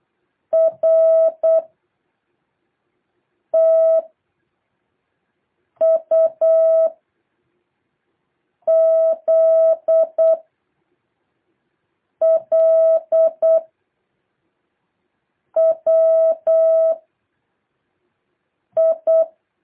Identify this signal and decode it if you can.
Morse code, heard as a keyed tone, about 8 wpm, 660 Hz. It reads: R T U Z L W I
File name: jh8cbh / 2023cw